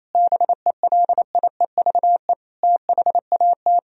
Morse code, heard as a keyed tone, about 28 wpm, 700 Hz. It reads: BELIEVE THAT
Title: Sentences from Top 1000 Words - Review Part 1 - 28wpm